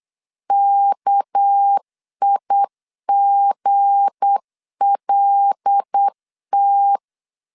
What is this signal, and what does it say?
Morse code, heard as a keyed tone, about 8 wpm, 800 Hz. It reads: KIGLT
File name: LEZIONI / 8 YJ45 / 8 YJ 8wpm 3313